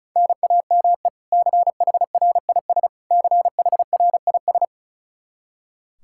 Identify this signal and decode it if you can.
Morse code, heard as a keyed tone, about 36 wpm, 700 Hz. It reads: NAME CHRIS CHRIS
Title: Practice QSO - Exchange 1 - Beginning - 35wpm